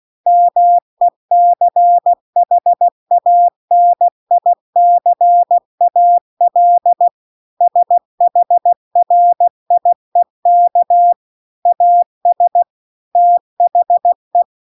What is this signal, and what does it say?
Morse code, heard as a keyed tone, about 16 wpm, 700 Hz. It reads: MECHANICAL SHRIEK AS THE
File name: CW-The-Big-Twoheaded-River-2-16WPM